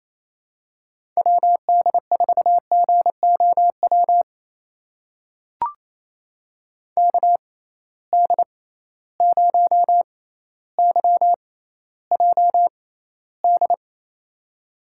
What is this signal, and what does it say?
Morse code, heard as a keyed tone, about 28 wpm, spaced as for 14 wpm, 700 Hz. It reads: WD4GOW KD0YJD
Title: USA Call Signs - 6x Character Spacing - 28wpm